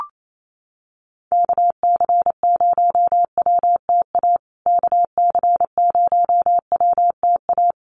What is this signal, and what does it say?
Morse code, heard as a keyed tone, about 28 wpm, 700 Hz. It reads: KC0WTA KC0WTA
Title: USA Call Signs Repeated 3x 28wpm